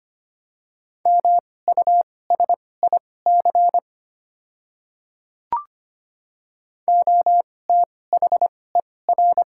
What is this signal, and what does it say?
Morse code, heard as a keyed tone, about 25 wpm, spaced as for 18 wpm, 700 Hz. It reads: MUSIC OTHER
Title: 2x Character Spacing - Top 500 Words - 25wpm